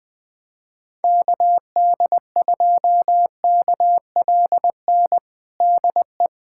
Morse code, heard as a keyed tone, about 20 wpm, 700 Hz. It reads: KD2KLN DE
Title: USA Callsign DE Callsign - 6 Characters - 20wpm